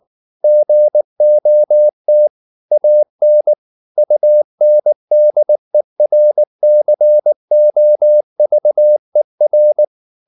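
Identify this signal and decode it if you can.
Morse code, heard as a keyed tone, about 19 wpm, 600 Hz. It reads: GOT AN UNDERCOVER
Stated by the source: Morse code at 19 wpm from CW med Gnister, reading Ken Follett's novel Never in English.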